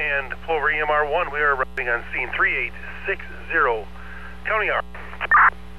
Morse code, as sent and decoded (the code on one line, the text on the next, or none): none